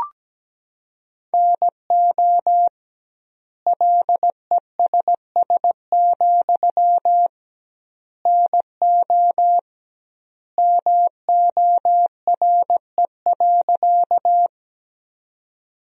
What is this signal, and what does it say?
Morse code, heard as a keyed tone, about 17 wpm, 700 Hz. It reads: NO LESS, NO MORE.
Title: 2x Word-Spacing - Sentences from Top 400 Words - 17wpm